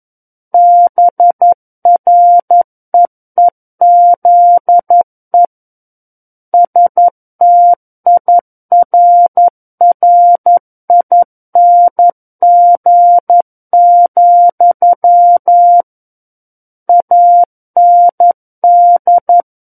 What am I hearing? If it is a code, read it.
Morse code, heard as a keyed tone, about 11 wpm, 700 Hz. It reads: BREEZE STIRRING, AND
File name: Book One / 02-Chapter 2 - 11 WPM